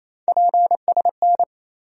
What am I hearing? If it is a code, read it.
Morse code, heard as a keyed tone, about 28 wpm, 700 Hz. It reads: PSN
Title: USA Call Sign Postfixes - 3 Letters 28wpm